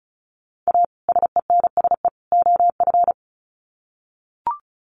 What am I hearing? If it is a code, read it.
Morse code, heard as a keyed tone, about 34 wpm, 700 Hz. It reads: A SENSE OF
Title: Sets of 3 Words 35wpm